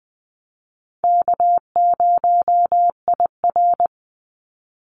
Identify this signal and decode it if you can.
Morse code, heard as a keyed tone, about 20 wpm, 700 Hz. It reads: K0IR
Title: Call Signs - 20wpm